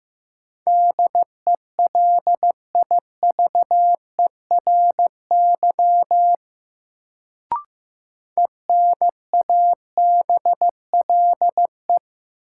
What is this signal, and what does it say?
Morse code, heard as a keyed tone, about 15 wpm, 700 Hz. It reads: DELIVERY ENABLE
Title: Sentences from Top 1500 Words - New Word Drill - 15wpm